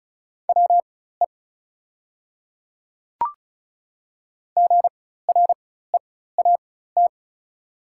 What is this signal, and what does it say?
Morse code, heard as a keyed tone, about 34 wpm, spaced as for 16 wpm, 700 Hz. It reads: WE GREAT
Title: 4x Character Spacing - Top 200 Words - 35wpm